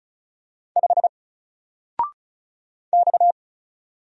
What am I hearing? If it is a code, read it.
Morse code, heard as a keyed tone, about 34 wpm, 700 Hz. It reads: 5 X
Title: Single Letter-Number 35wpm